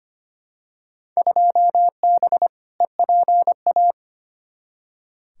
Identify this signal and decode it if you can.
Morse code, heard as a keyed tone, about 25 wpm, 700 Hz. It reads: 2B EPA
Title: Field Day Exchanges Encore - 25wpm